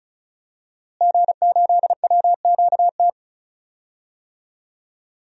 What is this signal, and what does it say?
Morse code, heard as a keyed tone, about 36 wpm, 700 Hz. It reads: G8WQT
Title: DXCC-100 Call Signs - 5 Characters - Repeated 2x - 35wpm